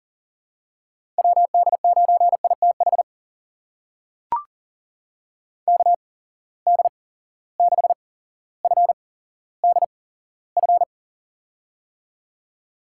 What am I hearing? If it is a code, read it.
Morse code, heard as a keyed tone, about 37 wpm, spaced as for 17 wpm, 700 Hz. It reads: WD9ITH KD6FDF
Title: USA Call Signs - 8x Character Spacing - 40wpm